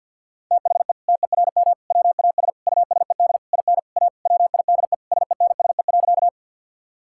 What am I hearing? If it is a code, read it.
Morse code, heard as a keyed tone, about 50 wpm, 700 Hz. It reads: THE TERM WAS USED IN A WIDE SENSE.
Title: Sentences from Top 1000 Words - Review Part 6 - 50wpm